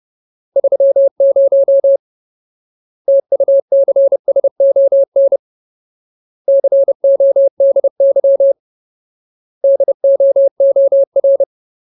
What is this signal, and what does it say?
Morse code, heard as a keyed tone, about 30 wpm, 550 Hz. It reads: 30 TUCSON CODY DOOR
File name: Ultimate Word List / 30WPM Ultimate